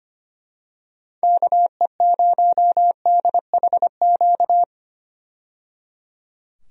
Morse code, heard as a keyed tone, about 25 wpm, 700 Hz. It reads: KE0DHQ
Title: USA Call Signs - 6 Characters - 25wpm